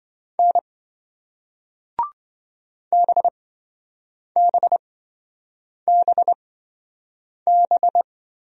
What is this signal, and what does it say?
Morse code, heard as a keyed tone, about 27 wpm, spaced as for 7 wpm, 700 Hz. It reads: N BBBB